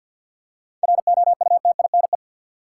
Morse code, heard as a keyed tone, about 49 wpm, 700 Hz. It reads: ROUTINE
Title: Seven Letter Words - 50wpm